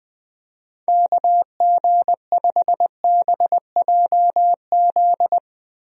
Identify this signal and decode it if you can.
Morse code, heard as a keyed tone, about 20 wpm, 700 Hz. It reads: KG5BJZ
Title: USA Call Signs - 6 Characters - Repeated 2x - 20wpm